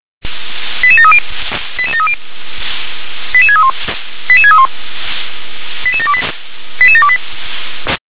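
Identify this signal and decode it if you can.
Five-tone selective call (ZVEI), heard as quick runs of five tones, tones of about 70 ms. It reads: 88730 88730 88631 88631 88728 88728
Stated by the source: ZVEI five-tone selective call received on an SDR.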